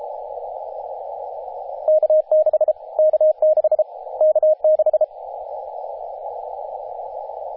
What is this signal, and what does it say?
Morse code, heard as a keyed tone, about 33 wpm, 620 Hz, with radio noise. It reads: K6 K6 K6
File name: t30cow-18-072mhz-cw